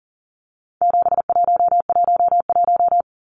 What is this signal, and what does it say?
Morse code, heard as a keyed tone, about 40 wpm, 700 Hz. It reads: 7111
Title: Numbers - 4-Digit 40wpm